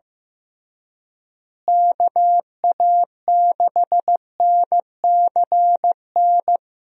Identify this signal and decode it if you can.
Morse code, heard as a keyed tone, about 15 wpm, 700 Hz. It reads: KA6NCN
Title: USA Call Signs - 6 Characters - Repeated 2x - 15wpm